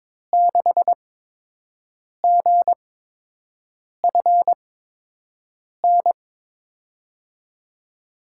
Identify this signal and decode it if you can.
Morse code, heard as a keyed tone, about 21 wpm, spaced as for 6 wpm, 700 Hz. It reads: 6GFN
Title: USA Call Signs - 6 Characters - 8x Character Spacing - 22wpm